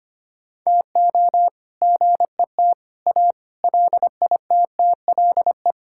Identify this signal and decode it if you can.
Morse code, heard as a keyed tone, about 25 wpm, 700 Hz. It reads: TO GET A LITTLE
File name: practice / 25wpm / All-Age Appropriate Jokes - 25wpm